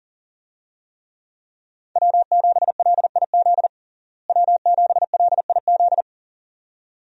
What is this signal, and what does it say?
Morse code, heard as a keyed tone, about 40 wpm, 700 Hz. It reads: W7LIZ W7LIZ
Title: USA Call Signs - 5 Characters - Repeated 2x - 40wpm